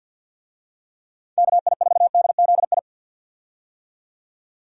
Morse code, heard as a keyed tone, about 53 wpm, 700 Hz. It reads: KI4DZI
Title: USA Call Signs - 6 Characters - 50wpm